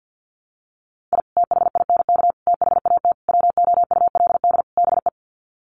Morse code, heard as a keyed tone, about 50 wpm, 700 Hz. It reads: I THINK THAT WOULD BE
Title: Sets of 5 Words Encore2 50wpm